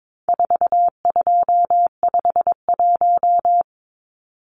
Morse code, heard as a keyed tone, about 22 wpm, 700 Hz. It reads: 4251
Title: Numbers - 4-Digit 22wpm